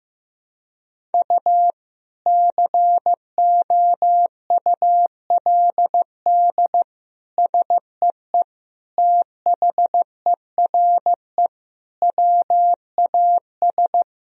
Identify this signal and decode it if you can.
Morse code, heard as a keyed tone, about 15 wpm, 700 Hz. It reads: U COULD SEE THERE WAS
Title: Sentences from Top 500 Words and Abbreviations 15wpm